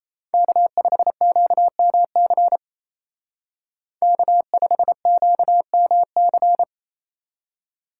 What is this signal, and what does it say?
Morse code, heard as a keyed tone, about 33 wpm, 700 Hz. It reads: K5QMC K5QMC
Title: Speed Racing - USA 1x3 Call Signs 28wpm